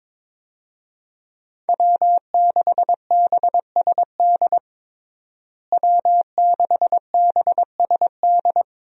Morse code, heard as a keyed tone, about 22 wpm, 700 Hz. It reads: W6BSD W6BSD
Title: USA Call Signs - 5 Characters - Repeated 3x - 22wpm